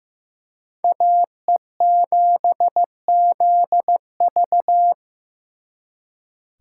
Morse code, heard as a keyed tone, about 18 wpm, 700 Hz. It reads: AE7ZV
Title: Speed Racing - USA 2x2 Call Signs 15wpm